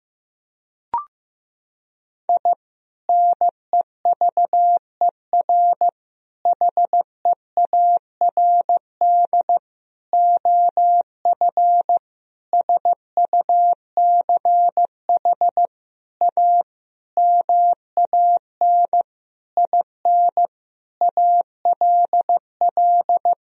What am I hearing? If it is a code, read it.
Morse code, heard as a keyed tone, about 15 wpm, 700 Hz. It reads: I NEVER HEARD OF SUCH A MAN IN ALL
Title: Sentences from Top 1000 Words - Review Part 3 - 15wpm